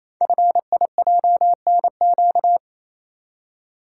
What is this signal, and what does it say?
Morse code, heard as a keyed tone, about 29 wpm, 700 Hz. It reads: FIJNQ
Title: Code Groups - 5 Letters - 28wpm